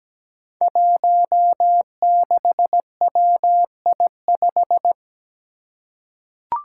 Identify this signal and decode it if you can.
Morse code, heard as a keyed tone, about 17 wpm, 700 Hz. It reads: 16WI5
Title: Five Challenging Letter-Number Combinations 17wpm